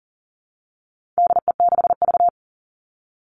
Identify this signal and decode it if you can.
Morse code, heard as a keyed tone, about 40 wpm, 700 Hz. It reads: DE6V
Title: All Cut Numbers - 4-Digit 40wpm